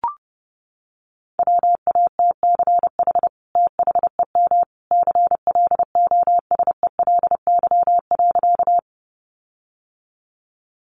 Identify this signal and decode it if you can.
Morse code, heard as a keyed tone, about 28 wpm, 700 Hz. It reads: WATCH THEM CLOSELY.
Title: Sentences from Top 1400 Words 30wpm